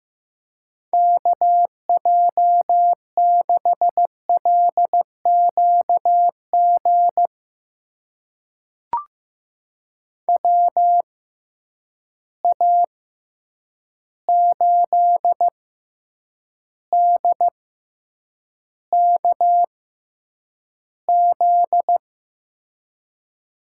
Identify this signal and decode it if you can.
Morse code, heard as a keyed tone, about 15 wpm, spaced as for 8 wpm, 700 Hz. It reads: KJ6LQG WA8DKZ
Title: USA Call Signs - 6 Characters - 6x Character Spacing - 15wpm